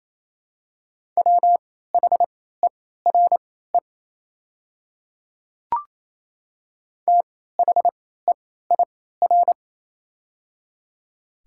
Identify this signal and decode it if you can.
Morse code, heard as a keyed tone, about 28 wpm, spaced as for 16 wpm, 700 Hz. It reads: WHERE THEIR
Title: 3x Character Spacing - Top 300 Words - 28wpm